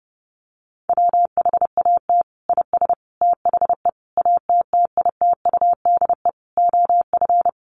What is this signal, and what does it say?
Morse code, heard as a keyed tone, about 30 wpm, 700 Hz. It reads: WHAT IS THE ATTITUDE OF
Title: Sentences from Top 1300 Words 30wpm